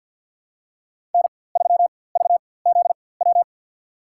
Morse code, heard as a keyed tone, about 47 wpm, spaced as for 23 wpm, 700 Hz. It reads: N3VZW
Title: USA Call Signs - 5 Characters - 4x Character Spacing - 50wpm